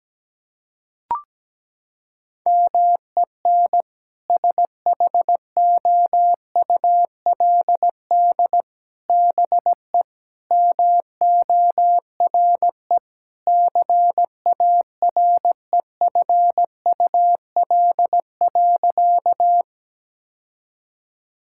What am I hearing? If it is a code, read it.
Morse code, heard as a keyed tone, about 17 wpm, 700 Hz. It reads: MEN SHOULD BE MORE CAREFUL.
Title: Sentences from Top 1600 Words 17wpm